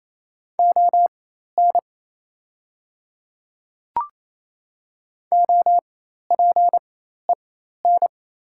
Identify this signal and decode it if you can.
Morse code, heard as a keyed tone, about 28 wpm, spaced as for 13 wpm, 700 Hz. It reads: ON OPEN
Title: 4x Character Spacing - Top 500 Words - 28wpm